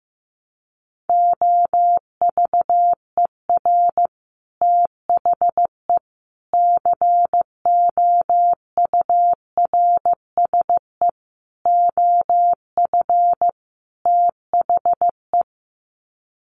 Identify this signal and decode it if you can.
Morse code, heard as a keyed tone, about 14 wpm, 700 Hz. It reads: OVER THE COURSE OF THE
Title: Sets of 5 Words 15wpm